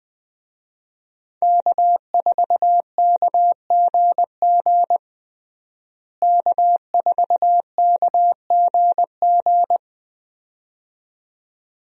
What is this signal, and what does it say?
Morse code, heard as a keyed tone, about 20 wpm, 700 Hz. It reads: K4KGG K4KGG
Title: USA Call Signs - 5 Characters - Repeated 2x - 20wpm